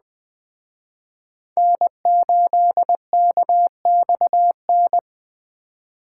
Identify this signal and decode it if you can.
Morse code, heard as a keyed tone, about 20 wpm, 700 Hz. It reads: N8KXN
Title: Speed Racing - USA 1x3 Call Signs 17wpm